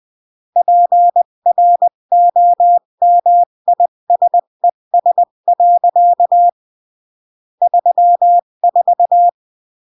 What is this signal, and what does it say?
Morse code, heard as a keyed tone, about 20 wpm, 700 Hz. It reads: PROMISES. 34
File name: CW / jodorowsky / 82 commandments 20-15